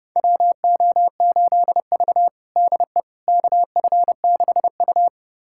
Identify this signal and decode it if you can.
Morse code, heard as a keyed tone, about 30 wpm, 700 Hz. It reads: WO8V DE KF6U
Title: USA Callsign DE Callsign - 4 Characters - 30wpm